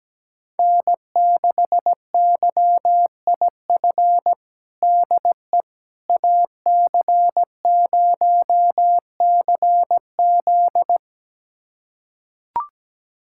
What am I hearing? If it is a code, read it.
Morse code, heard as a keyed tone, about 17 wpm, 700 Hz. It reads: N6YIF DE AC0CZ
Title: USA Callsign DE Callsign - 5 Characters - 17wpm